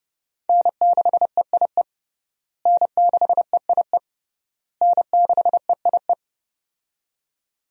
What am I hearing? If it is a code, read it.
Morse code, heard as a keyed tone, about 30 wpm, 700 Hz. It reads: N6EIE N6EIE N6EIE
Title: USA Call Signs - 5 Characters - Repeated 3x - 30wpm